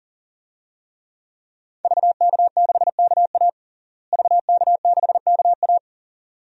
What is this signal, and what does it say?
Morse code, heard as a keyed tone, about 40 wpm, 700 Hz. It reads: VK6KA VK6KA
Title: DXCC-100 Call Signs - 5 Characters - Repeated 2x - 40wpm